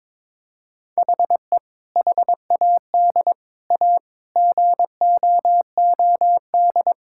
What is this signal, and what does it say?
Morse code, heard as a keyed tone, about 22 wpm, 700 Hz. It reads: HE HAD A GOOD
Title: Sentences from Top 1000 Words - Review Part 6 - 22wpm